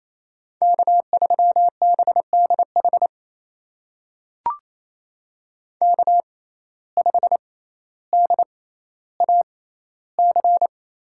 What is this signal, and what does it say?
Morse code, heard as a keyed tone, about 28 wpm, spaced as for 14 wpm, 700 Hz. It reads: K3BDH K5DAC
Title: USA Call Signs - 5 Characters - 6x Character Spacing - 28wpm